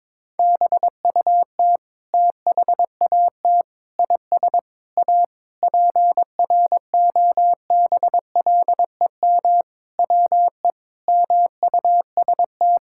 Morse code, heard as a keyed tone, about 21 wpm, 700 Hz. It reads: BUT THAT IS A PROBLEM WE MUST
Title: Sentences from Top 1000 Words - Review Part 6 - 22wpm